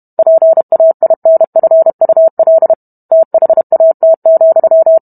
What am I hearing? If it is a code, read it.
Morse code, heard as a keyed tone, about 32 wpm, 650 Hz. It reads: PAINFUL THAT,